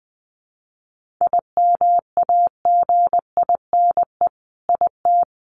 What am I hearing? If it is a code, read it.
Morse code, heard as a keyed tone, about 20 wpm, 700 Hz. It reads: IMAGINE IT
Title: Sentences from Top 1400 Words 20wpm